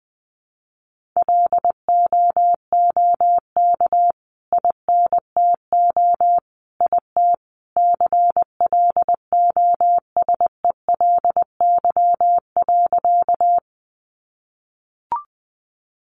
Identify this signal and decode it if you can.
Morse code, heard as a keyed tone, about 20 wpm, 700 Hz. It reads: LOOK INTO IT CLOSELY.
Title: Sentences from Top 1400 Words 20wpm